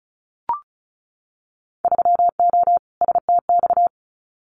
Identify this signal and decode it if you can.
Morse code, heard as a keyed tone, about 36 wpm, 690 Hz. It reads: 3O STX
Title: Winter Field Day Exchanges 35wpm